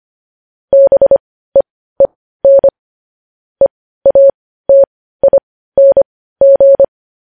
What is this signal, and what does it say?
Morse code, heard as a keyed tone, about 25 wpm, spaced as for 15 wpm, 550 Hz. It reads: BEEN EATING
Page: news headlines in morse code at 15 wpm